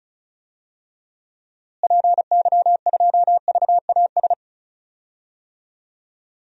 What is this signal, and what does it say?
Morse code, heard as a keyed tone, about 34 wpm, 700 Hz. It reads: PY2VAS
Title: DXCC-100 Call Signs - 6 Characters - Repeated 2x - 35wpm